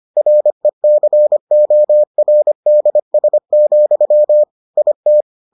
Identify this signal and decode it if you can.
Morse code, heard as a keyed tone, about 25 wpm, 600 Hz. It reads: RECORDS, IT